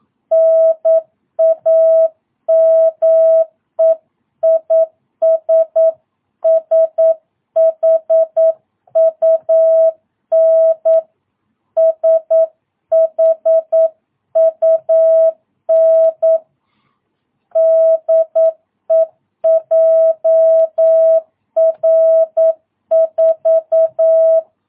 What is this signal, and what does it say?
Morse code, heard as a keyed tone, about 9 wpm, 660 Hz. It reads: NAME IS SHUN SHUN DE JR4
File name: jh8cbh / 2022cw